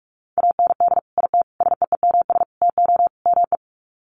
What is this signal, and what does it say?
Morse code, heard as a keyed tone, about 44 wpm, 700 Hz. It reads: AND IT SEEMS TO ME
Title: Sets of 5 Words Encore2 45wpm